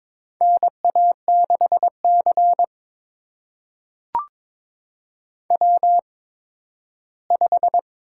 Morse code, heard as a keyed tone, about 22 wpm, spaced as for 13 wpm, 700 Hz. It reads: NA6C W5
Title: USA Call Signs - 4 Characters - 8x Character Spacing - 22wpm